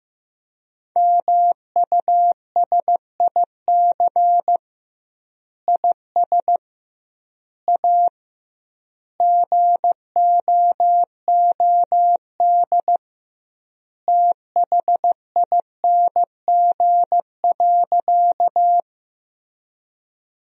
Morse code, heard as a keyed tone, about 15 wpm, 700 Hz. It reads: MUSIC IS A GOOD THING.
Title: 2x Word-Spacing - Sentences from Top 300 Words - 15wpm